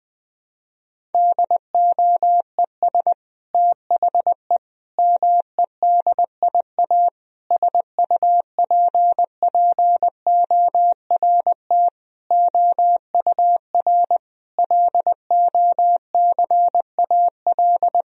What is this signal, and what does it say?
Morse code, heard as a keyed tone, about 20 wpm, 700 Hz. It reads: DOES THE MEDIA SUPPORT OUR LOCAL
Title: Sentences from Top 1000 Words - Review Part 4 - 20wpm